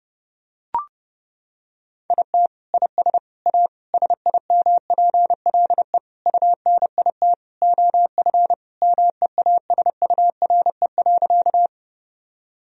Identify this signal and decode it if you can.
Morse code, heard as a keyed tone, about 30 wpm, 700 Hz. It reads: IT IS A SIMPLE UNIT OF MEASURE.
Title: Sentences from Top 1000 Words - Review Part 2 - 30wpm